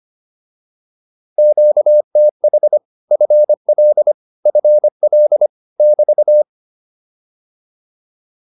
Morse code, heard as a keyed tone, about 25 wpm, 600 Hz. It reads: QTH FL FL <BT>